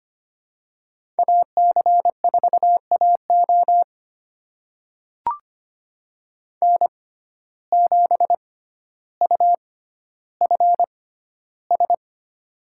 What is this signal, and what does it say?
Morse code, heard as a keyed tone, about 25 wpm, spaced as for 13 wpm, 700 Hz. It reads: AC4AO N7UFS
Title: USA Call Signs - 5 Characters - 6x Character Spacing - 25wpm